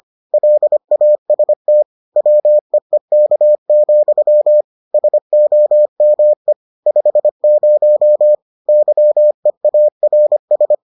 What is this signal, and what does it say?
Morse code, heard as a keyed tone, about 25 wpm, 600 Hz. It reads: LAST WEEK, SOME 50 YEARS